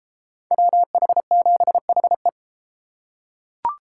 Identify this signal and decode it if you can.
Morse code, heard as a keyed tone, about 30 wpm, 700 Hz. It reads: WH7HE